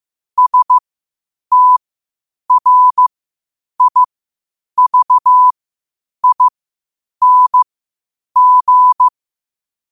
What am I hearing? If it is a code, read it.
Morse code, heard as a keyed tone, about 15 wpm, spaced as for 8 wpm, 1000 Hz. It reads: STRIVING
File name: qotdfiles / 8wpm